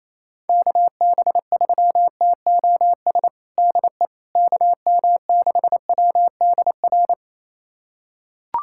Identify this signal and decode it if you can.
Morse code, heard as a keyed tone, about 28 wpm, 700 Hz. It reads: KB3TOS DE KM6WDR
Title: USA Callsign DE Callsign - 6 Characters - 28wpm